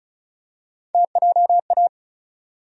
Morse code, heard as a keyed tone, about 36 wpm, 700 Hz. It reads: TJA
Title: USA Call Sign Postfixes - 3 Letters 35wpm